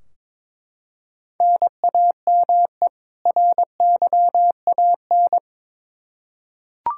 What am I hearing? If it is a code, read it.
Morse code, heard as a keyed tone, about 22 wpm, 700 Hz. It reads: NAME RYAN
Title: Practice QSO - Exchange 1 - Beginning - 22wpm